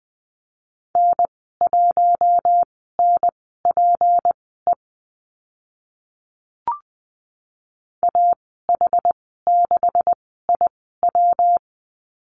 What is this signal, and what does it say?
Morse code, heard as a keyed tone, about 19 wpm, spaced as for 14 wpm, 700 Hz. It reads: N1NPE AH6IW
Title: 193 Lesson 31g - Letter v - Call Signs 20wpm